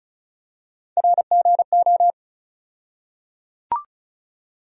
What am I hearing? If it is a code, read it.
Morse code, heard as a keyed tone, about 36 wpm, 700 Hz. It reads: RGO
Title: USA Call Sign Postfixes - 3 Letters 35wpm